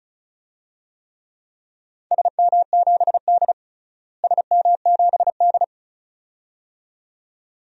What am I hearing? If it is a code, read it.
Morse code, heard as a keyed tone, about 36 wpm, 700 Hz. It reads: SM7D SM7D
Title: DXCC-100 Call Signs - 4 Characters - Repeated 2x - 35wpm